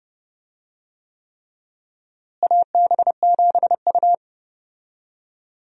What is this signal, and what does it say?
Morse code, heard as a keyed tone, about 31 wpm, 700 Hz. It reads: AB7U
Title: USA Call Signs - 4 Characters - Repeated 3x - 30wpm